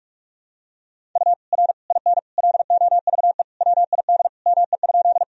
Morse code, heard as a keyed tone, about 41 wpm, 700 Hz. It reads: U R IN LOVE WID ME?
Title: Sentences from Top 400 Words and Abbreviations 45wpm